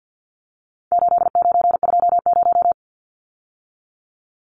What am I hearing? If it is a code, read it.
Morse code, heard as a keyed tone, about 50 wpm, 700 Hz. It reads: 8920
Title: Numbers - 4-Digit 50wpm